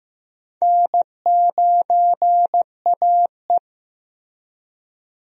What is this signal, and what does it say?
Morse code, heard as a keyed tone, about 15 wpm, 700 Hz. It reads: N9AE
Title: USA Call Signs - 4 Characters - Repeated 2x - 15wpm